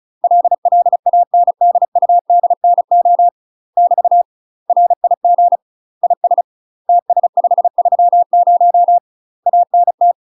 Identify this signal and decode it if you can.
Morse code, heard as a keyed tone, about 36 wpm, 700 Hz. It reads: LLANDUDNO <BT> RIG IS TS530 ANT
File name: UK MASTER QSOs TRAINING / 35WPM / UK MASTER QSOs 0041